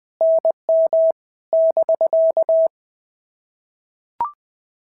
Morse code, heard as a keyed tone, about 20 wpm, 650 Hz. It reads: NM <BK>
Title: POTA QSO Practice - 20wpm